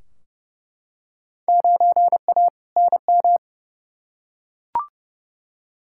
Morse code, heard as a keyed tone, about 31 wpm, 700 Hz. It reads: 9A NM
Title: Field Day Exchanges Encore - 30wpm